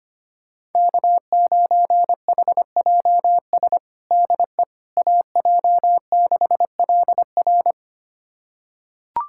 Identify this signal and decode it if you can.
Morse code, heard as a keyed tone, about 25 wpm, 700 Hz. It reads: K9HJS DE AJ6LR
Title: USA Callsign DE Callsign - 5 Characters - 25wpm